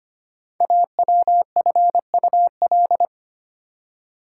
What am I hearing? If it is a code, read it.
Morse code, heard as a keyed tone, about 25 wpm, 700 Hz. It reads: AWFUL
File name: 5 Letter Words - First 3 Letters Given - 25wpm